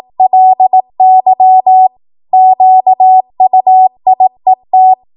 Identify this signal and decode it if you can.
Morse code, heard as a keyed tone, about 18 wpm, 750 Hz. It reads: LY QUIET
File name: Morse / 18WPM